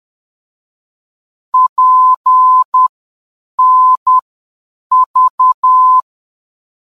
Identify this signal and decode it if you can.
Morse code, heard as a keyed tone, about 10 wpm, spaced as for 7 wpm, 1020 Hz. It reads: PNV
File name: PNV-VOR DME